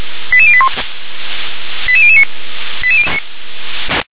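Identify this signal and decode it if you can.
Five-tone selective call (ZVEI), heard as quick runs of five tones, tones of about 70 ms. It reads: 88081 88009
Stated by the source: ZVEI five-tone selective call received on SDR.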